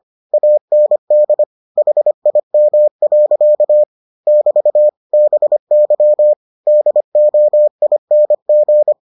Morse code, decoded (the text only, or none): AND HIM. <BT> BY DOING